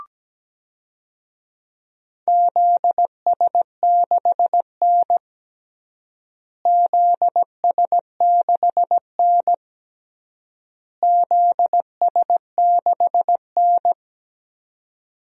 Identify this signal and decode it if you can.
Morse code, heard as a keyed tone, about 17 wpm, 700 Hz. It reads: ZS6N ZS6N ZS6N
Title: DXCC-100 Call Signs - 4 Characters - Repeated 3x - 17wpm